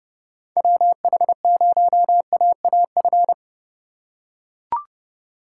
Morse code, heard as a keyed tone, about 31 wpm, 700 Hz. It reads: WH0AAF